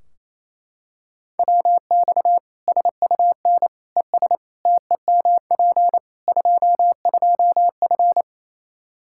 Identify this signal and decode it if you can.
Morse code, heard as a keyed tone, about 28 wpm, 700 Hz. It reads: WX SUN ES TEMP 22F
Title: Practice QSO - Exchange 2 - Info - 28wpm